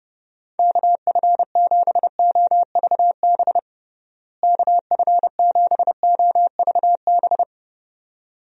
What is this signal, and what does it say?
Morse code, heard as a keyed tone, about 30 wpm, 700 Hz. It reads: KF7OVB KF7OVB
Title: USA Call Signs - 6 Characters - Repeated 2x - 30wpm